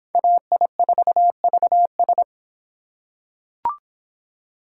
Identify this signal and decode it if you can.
Morse code, heard as a keyed tone, about 24 wpm, 700 Hz. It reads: AI4VS